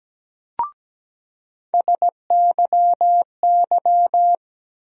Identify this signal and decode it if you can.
Morse code, heard as a keyed tone, about 17 wpm, 700 Hz. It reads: SYY
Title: USA Call Sign Postfixes - 3 Letters 17wpm